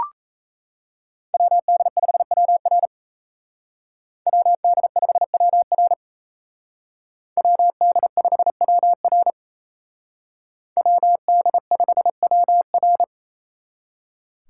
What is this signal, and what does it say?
Morse code, heard as a keyed tone, about 33 wpm, 700 Hz. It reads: WD5WR WD5WR WD5WR WD5WR
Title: Speed Racing - USA 2x2 Call Signs 28wpm